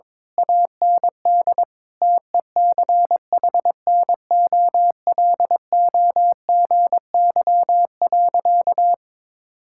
Morse code, heard as a keyed tone, about 22 wpm, 700 Hz. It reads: AND TECHNOLOGY.
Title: Sentences from Top 1000 Words - Review Part 6 - 22wpm